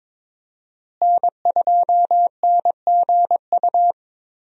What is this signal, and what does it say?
Morse code, heard as a keyed tone, about 21 wpm, 700 Hz. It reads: N2NGU
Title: USA Call Signs - 5 Characters - Repeated 3x - 22wpm